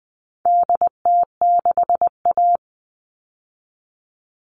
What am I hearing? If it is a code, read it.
Morse code, heard as a keyed tone, about 20 wpm, 700 Hz. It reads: DT6A
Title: All Cut Numbers - 4-Digit 20wpm